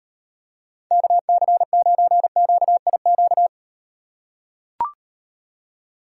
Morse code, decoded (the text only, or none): KC9QIQ